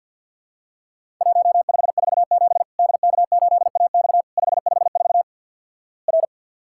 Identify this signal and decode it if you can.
Morse code, heard as a keyed tone, about 50 wpm, 700 Hz. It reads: 1547 DK8AX 554 R
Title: Worked All Europe DX Contest - QTC Exchange - 50wpm